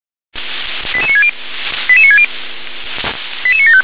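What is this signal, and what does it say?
Five-tone selective call (ZVEI), heard as quick runs of five tones, tones of about 70 ms. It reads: 88960 88960 88975
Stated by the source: ZVEI five-tone selective call received by SDR.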